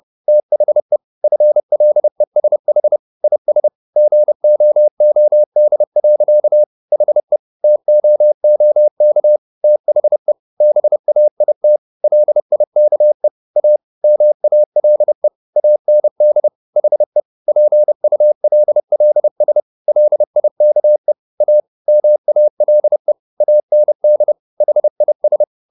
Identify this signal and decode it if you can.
Morse code, heard as a keyed tone, about 30 wpm, 600 Hz. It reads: THE FLESH IS GOOD. HE TOOK THE BAIT LIKE A MALE AND HE PULLS LIKE A MALE AND HIS
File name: the old man and the sea / the old man and the sea 1 30wpm